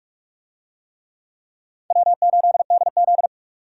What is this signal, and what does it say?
Morse code, heard as a keyed tone, about 42 wpm, 690 Hz. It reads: W8DZ